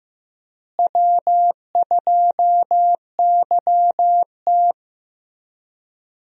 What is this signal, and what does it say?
Morse code, heard as a keyed tone, about 15 wpm, 700 Hz. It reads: W2YT
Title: USA Call Signs - 4 Characters - Repeated 3x - 15wpm